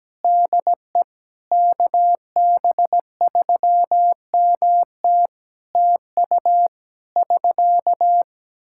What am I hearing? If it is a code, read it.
Morse code, heard as a keyed tone, about 17 wpm, 700 Hz. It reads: DE KB3MT TU <SK>
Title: K1USN SST QSO Practice 17wpm